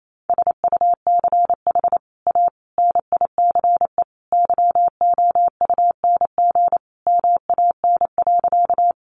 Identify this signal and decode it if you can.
Morse code, heard as a keyed tone, about 28 wpm, 700 Hz. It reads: SUCH A NICE YOUNG MAN.
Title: Sentences from Top 1100 Words 28wpm